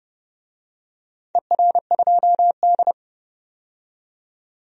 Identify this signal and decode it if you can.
Morse code, heard as a keyed tone, about 29 wpm, 700 Hz. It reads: ER2D
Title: DXCC-100 Call Signs - 4 Characters - 30wpm